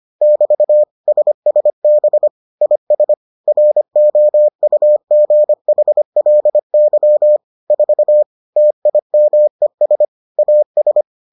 Morse code, heard as a keyed tone, about 25 wpm, 600 Hz. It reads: <BT> SSB IS ROUGHLY 4 TIMES AS